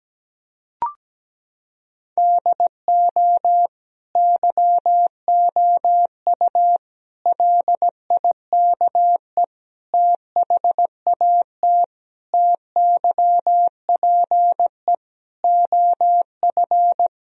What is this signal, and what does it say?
Morse code, heard as a keyed tone, about 17 wpm, 700 Hz. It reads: DO YOU LIKE THAT TYPE OF
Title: Sentences from Top 1000 Words - Review Part 3 - 17wpm